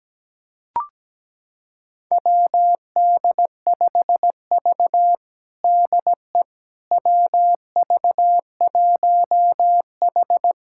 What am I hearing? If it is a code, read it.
Morse code, heard as a keyed tone, about 17 wpm, 700 Hz. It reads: WD5V DE WV1H
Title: USA Callsign DE Callsign - 4 Characters - 17wpm